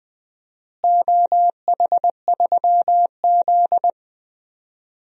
Morse code, heard as a keyed tone, about 20 wpm, 700 Hz. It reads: OH3Z
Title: DXCC-100 Call Signs - 4 Characters - Repeated 2x - 20wpm